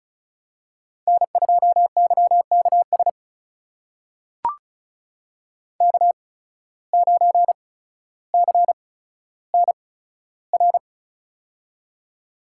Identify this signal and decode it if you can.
Morse code, heard as a keyed tone, about 36 wpm, spaced as for 15 wpm, 700 Hz. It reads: N2YKS K9CNR
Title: USA Call Signs - 5 Characters - 8x Character Spacing - 35wpm